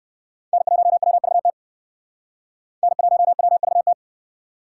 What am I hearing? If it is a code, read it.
Morse code, heard as a keyed tone, about 69 wpm, 700 Hz. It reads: N1WVT N1WVT